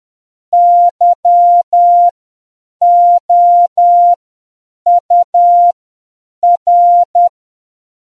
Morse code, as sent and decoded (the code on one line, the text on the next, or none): -.-- --- ..- .-.
YOUR